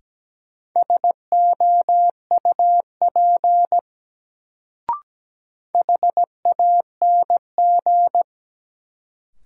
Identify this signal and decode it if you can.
Morse code, heard as a keyed tone, about 17 wpm, 700 Hz. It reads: SOUP HANG
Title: Sentences from Top 1600 Words - New Word Drill Alt - 17wpm